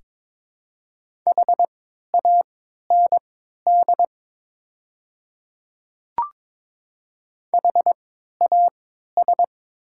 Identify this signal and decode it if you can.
Morse code, heard as a keyed tone, about 22 wpm, spaced as for 12 wpm, 700 Hz. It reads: HAND HAS
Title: 3x Character Spacing - Top 300 Words - 22wpm